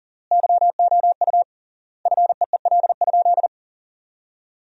Morse code, heard as a keyed tone, about 40 wpm, 700 Hz. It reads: YOU FEEL?
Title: 3x Word-Spacing - Sentences from Top 400 Words - 40wpm